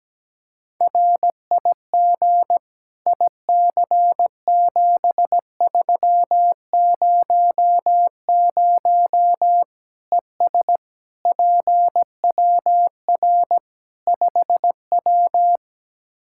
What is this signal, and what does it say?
Morse code, heard as a keyed tone, about 17 wpm, 700 Hz. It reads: RIG IC7300 ES PWR 5W
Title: Practice QSO - Exchange 2 - Info - 17wpm